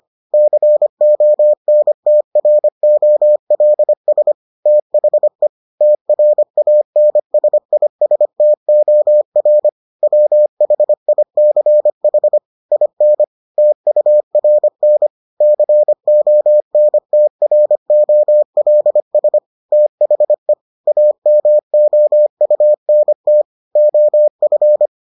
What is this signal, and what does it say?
Morse code, heard as a keyed tone, about 25 wpm, 600 Hz. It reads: CONTROLS THE TRANSISTOR WHICH IN TURN CONTROLS THE AMOUNT OF